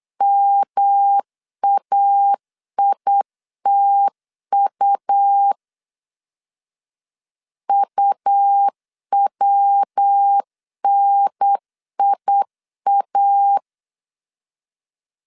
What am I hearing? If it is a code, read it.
Morse code, heard as a keyed tone, about 8 wpm, 800 Hz. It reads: MAITU UWNIA
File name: CW / DET / 1 3 1500C S B 8wpm 4141